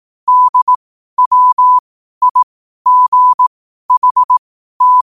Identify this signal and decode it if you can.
Morse code, heard as a keyed tone, about 18 wpm, spaced as for 12 wpm, 1000 Hz. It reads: DWIGHT
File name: qotdfiles / 12wpm